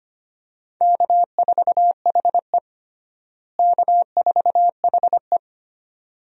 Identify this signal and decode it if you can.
Morse code, heard as a keyed tone, about 25 wpm, 700 Hz. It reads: K4HE K4HE
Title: USA Call Signs - 4 Characters - Repeated 2x - 25wpm